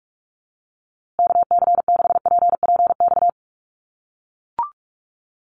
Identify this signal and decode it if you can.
Morse code, heard as a keyed tone, about 44 wpm, 700 Hz. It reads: KC6PPX
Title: USA Call Signs Repeated 3x 45wpm